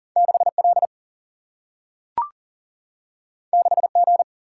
Morse code, heard as a keyed tone, about 40 wpm, 700 Hz. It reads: 6L 6G
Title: Single-Letter Single-Number 40wpm